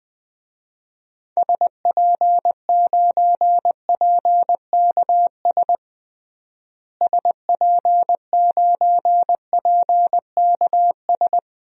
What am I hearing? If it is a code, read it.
Morse code, heard as a keyed tone, about 20 wpm, 700 Hz. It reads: SP9PKS SP9PKS
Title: DXCC-100 Call Signs - 6 Characters - Repeated 2x - 20wpm